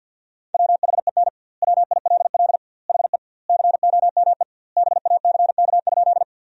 Morse code, heard as a keyed tone, about 52 wpm, 700 Hz. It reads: WHEN WILL HE COME BACK?
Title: 2x Word-Spacing - Sentences from Top 200 Words - 50wpm